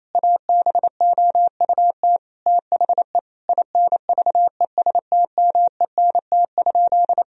Morse code, heard as a keyed tone, about 28 wpm, 700 Hz. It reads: ABOUT THE INVESTMENT?
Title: Sentences from Top 1000 Words - Review Part 4 - 28wpm